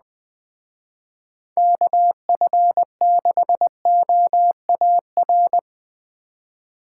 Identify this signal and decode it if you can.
Morse code, heard as a keyed tone, about 20 wpm, 700 Hz. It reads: KF6OAR
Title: Speed Racing - USA 2x3 Call Signs 17wpm